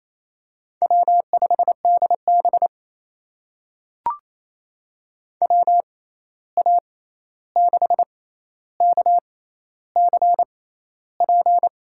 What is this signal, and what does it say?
Morse code, heard as a keyed tone, about 27 wpm, spaced as for 13 wpm, 700 Hz. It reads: W5DB WA6KCP
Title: USA Call Signs - 6x Character Spacing - 28wpm